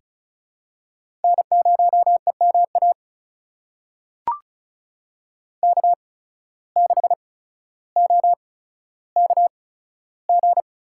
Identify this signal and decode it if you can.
Morse code, heard as a keyed tone, about 36 wpm, spaced as for 15 wpm, 700 Hz. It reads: N0EMA K6OKG
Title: USA Call Signs - 5 Characters - 8x Character Spacing - 35wpm